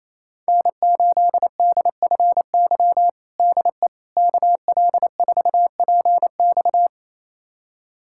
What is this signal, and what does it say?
Morse code, heard as a keyed tone, about 28 wpm, 700 Hz. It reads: N8DFY DE KL4PX
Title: USA Callsign DE Callsign - 5 Characters - 28wpm